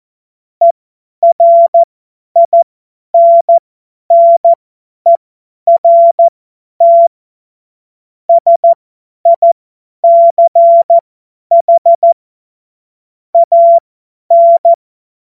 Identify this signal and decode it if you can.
Morse code, heard as a keyed tone, about 14 wpm, spaced as for 10 wpm, 680 Hz. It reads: ERINNERT SICH AN
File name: Morsekurs / Witze / Witz 18. April 2023 50 F70